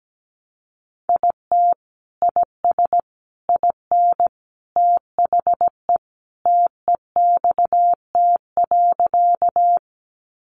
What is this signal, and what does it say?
Morse code, heard as a keyed tone, about 17 wpm, 700 Hz. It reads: IT IS IN THE TEXT.
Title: Sentences from Top 1200 Words 17wpm